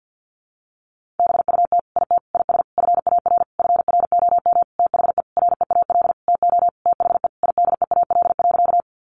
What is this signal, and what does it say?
Morse code, heard as a keyed tone, about 50 wpm, 700 Hz. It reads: BUT IT IS FAR FROM THE REAL TO THE IDEAL.
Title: Sentences from Top 1400 Words 50wpm